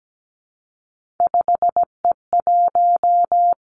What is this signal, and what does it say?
Morse code, heard as a keyed tone, about 17 wpm, 700 Hz. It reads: 5E1
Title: Three Challenging Letter-Number Combinations 17wpm